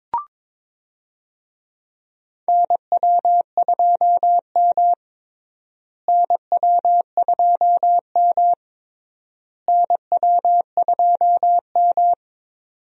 Morse code, heard as a keyed tone, about 22 wpm, 700 Hz. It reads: NW2M NW2M NW2M
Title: USA Call Signs - 4 Characters - Repeated 3x - 22wpm